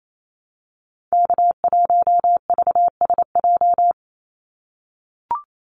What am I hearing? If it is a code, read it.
Morse code, heard as a keyed tone, about 28 wpm, 700 Hz. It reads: K1VSJ
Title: Call Signs - 28wpm